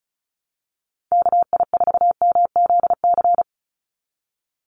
Morse code, heard as a keyed tone, about 34 wpm, 700 Hz. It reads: KI4MZC
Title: Call Signs Repeated 2x - 35wpm